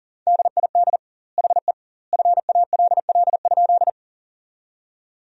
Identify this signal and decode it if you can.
Morse code, heard as a keyed tone, about 40 wpm, 700 Hz. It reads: DID HE FALL?
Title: 2x Word-Spacing - Sentences from Top 500 Words - 40wpm